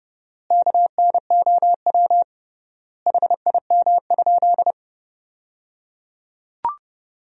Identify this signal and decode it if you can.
Morse code, heard as a keyed tone, about 30 wpm, 700 Hz. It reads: KNOW HIM?
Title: 3x Word-Spacing - Sentences from Top 100 Words - 30wpm